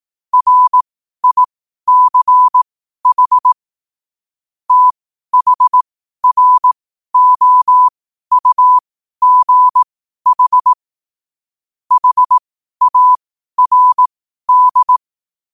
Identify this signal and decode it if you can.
Morse code, heard as a keyed tone, about 18 wpm, spaced as for 12 wpm, 1000 Hz. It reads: RICH THROUGH HARD
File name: 12wpm